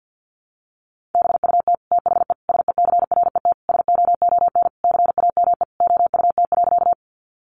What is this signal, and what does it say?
Morse code, heard as a keyed tone, about 50 wpm, 700 Hz. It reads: BUT THE SECRET SOON CAME OUT.
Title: Sentences from Top 1200 Words 50wpm